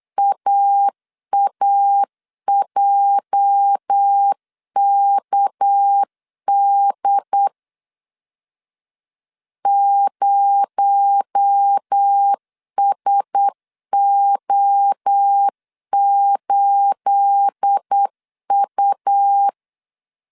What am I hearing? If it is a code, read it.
Morse code, heard as a keyed tone, about 8 wpm, 800 Hz. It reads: AAJKD 0SO8U
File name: CW / DET / 1 9 1500C S B 8wpm 5624